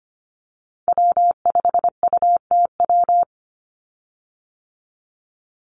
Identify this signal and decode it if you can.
Morse code, heard as a keyed tone, about 25 wpm, 700 Hz. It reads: W5UTW
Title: Call Signs Repeated 2x - 25wpm